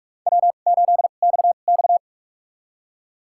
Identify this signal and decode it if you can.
Morse code, heard as a keyed tone, about 46 wpm, spaced as for 33 wpm, 700 Hz. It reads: W8XX
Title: USA Call Signs - 2x Character Spacing - 45wpm